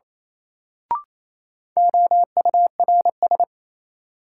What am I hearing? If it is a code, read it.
Morse code, heard as a keyed tone, about 27 wpm, 700 Hz. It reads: OURS
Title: Sentences from Top 1300 Words - New Word Drill Alt - 28wpm